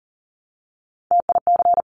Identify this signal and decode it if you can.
Morse code, heard as a keyed tone, about 42 wpm, 700 Hz. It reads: TIC
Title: Top Three Letter Combinations in Words 40wpm